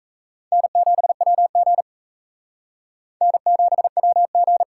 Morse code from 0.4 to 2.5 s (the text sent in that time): N7WG